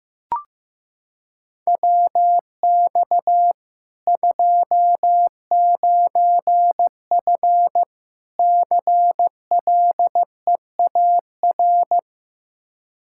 Morse code, heard as a keyed tone, about 16 wpm, 700 Hz. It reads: WX 29F CLEAR
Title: Practice QSO - Exchange 2 - Info - 15wpm